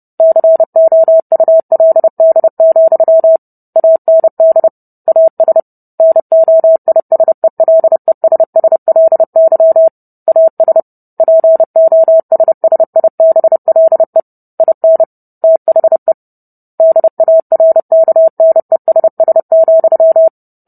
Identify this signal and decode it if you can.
Morse code, heard as a keyed tone, about 30 wpm, 650 Hz. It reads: COULD, AND AS NOISELESSLY AS POSSIBLE IN THE DARKNESS,